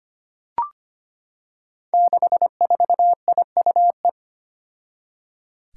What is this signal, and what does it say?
Morse code, heard as a keyed tone, about 25 wpm, 700 Hz. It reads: 64IUE
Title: Five Challenging Letter-Number Combinations 25wpm